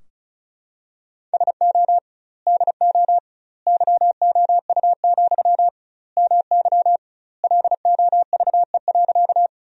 Morse code, heard as a keyed tone, about 34 wpm, 700 Hz. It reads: SO DO YOU, MY LOVE.
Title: 2x Word-Spacing - Sentences from Top 400 Words - 35wpm